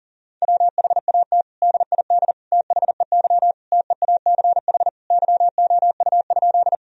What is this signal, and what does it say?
Morse code, heard as a keyed tone, about 40 wpm, 700 Hz. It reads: WHAT DID THEY TEACH YOU?
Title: Sentences from Top 1000 Words - Review Part 2 - 40wpm